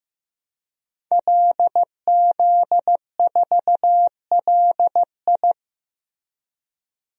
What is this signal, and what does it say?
Morse code, heard as a keyed tone, about 15 wpm, 700 Hz. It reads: LZ4LI
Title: DXCC-100 Call Signs - 5 Characters - Repeated 3x - 15wpm